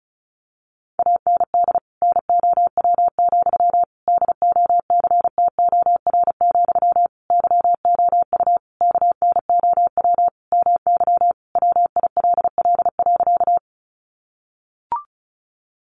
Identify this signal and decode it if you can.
Morse code, heard as a keyed tone, about 34 wpm, 700 Hz. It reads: AND NOW, DOCTOR, YOU KNOW MY WILL.
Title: Sentences from Top 1500 Words 35wpm